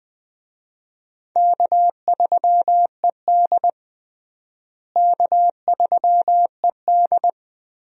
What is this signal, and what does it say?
Morse code, heard as a keyed tone, about 20 wpm, 700 Hz. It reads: K3ED K3ED
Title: USA Call Signs - 4 Characters - Repeated 3x - 20wpm